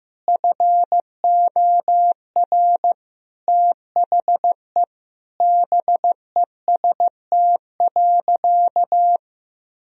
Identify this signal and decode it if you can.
Morse code, heard as a keyed tone, about 15 wpm, 700 Hz. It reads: FOR THE BEST.
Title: Sentences from Top 1000 Words - Review Part 3 - 15wpm